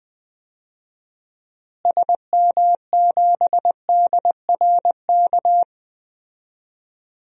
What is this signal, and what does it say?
Morse code, heard as a keyed tone, about 21 wpm, 700 Hz. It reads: SM7DRK
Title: DXCC-100 Call Signs - 6 Characters - Repeated 2x - 20wpm